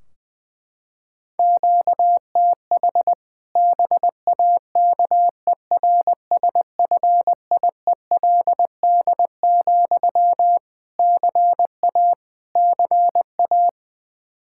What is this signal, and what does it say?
Morse code, heard as a keyed tone, about 21 wpm, 700 Hz. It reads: QTH BAKERSFIELD, CA CA
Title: Practice QSO - Exchange 1 - Beginning - 20wpm